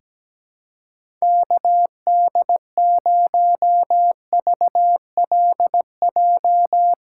Morse code, heard as a keyed tone, about 17 wpm, 700 Hz. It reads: KD0VLJ
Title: USA Callsign DE Callsign - 6 Characters - 17wpm